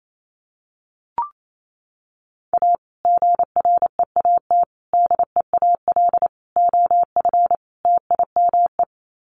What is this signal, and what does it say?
Morse code, heard as a keyed tone, about 28 wpm, 700 Hz. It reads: A GREAT DEAL OF TIME
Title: Sets of 5 Words Encore2 28wpm